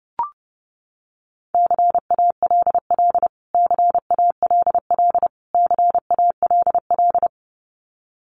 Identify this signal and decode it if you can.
Morse code, heard as a keyed tone, about 30 wpm, 700 Hz. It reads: CALL CALL CALL
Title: Top 200 Words Repeated 30wpm